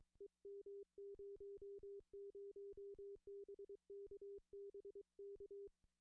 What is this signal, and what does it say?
Morse code, heard as a keyed tone, about 23 wpm, 400 Hz. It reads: EM00 BK BK
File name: contests / june-vhf-k5tr- -1300